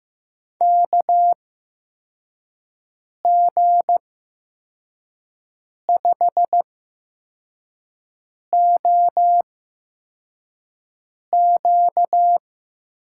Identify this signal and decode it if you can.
Morse code, heard as a keyed tone, about 15 wpm, spaced as for 4 wpm, 700 Hz. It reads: KG5OQ
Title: USA Call Signs - 6 Characters - 8x Character Spacing - 15wpm